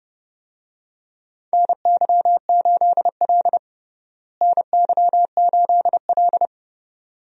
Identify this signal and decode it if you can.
Morse code, heard as a keyed tone, about 30 wpm, 700 Hz. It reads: NY8L NY8L
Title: USA Call Signs - 4 Characters - Repeated 2x - 30wpm